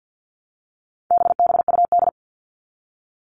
Morse code, heard as a keyed tone, about 50 wpm, 700 Hz. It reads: BBUD
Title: All Cut Numbers - 4-Digit 50wpm